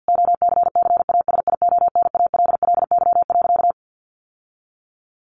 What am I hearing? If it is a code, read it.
Morse code, heard as a keyed tone, about 50 wpm, 700 Hz. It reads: OCCASIONALLY.
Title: Sentences from Top 1300 Words 50wpm